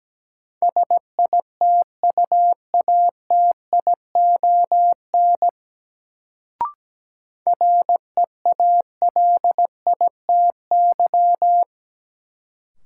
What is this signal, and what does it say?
Morse code, heard as a keyed tone, about 17 wpm, 700 Hz. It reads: SITUATION REALITY
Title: Sentences from Top 1200 Words - New Word Drill - 17wpm